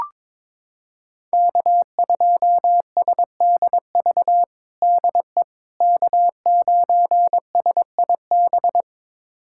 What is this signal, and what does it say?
Morse code, heard as a keyed tone, about 22 wpm, 700 Hz. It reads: K2SDV DE K9SIB
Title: USA Callsign DE Callsign - 5 Characters - 22wpm